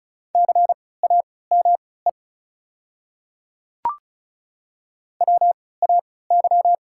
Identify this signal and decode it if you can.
Morse code, heard as a keyed tone, about 36 wpm, spaced as for 20 wpm, 700 Hz. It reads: CAME WAY
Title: 3x Character Spacing - Top 200 Words - 35wpm